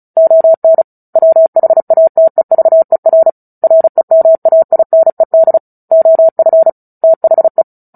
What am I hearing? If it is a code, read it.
Morse code, heard as a keyed tone, about 34 wpm, 650 Hz. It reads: ON WHATEVER REMAINED OF THE
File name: Book Two / 09-Chapter 9 - 35 WPM